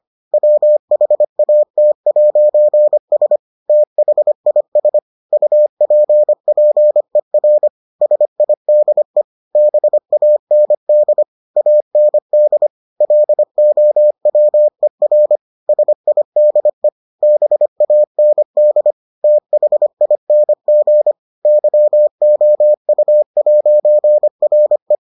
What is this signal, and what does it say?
Morse code, heard as a keyed tone, about 25 wpm, 600 Hz. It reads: WHAT'S THIS UPPER SIDE BAND AND LOWER SIDE BAND THING YOU'RE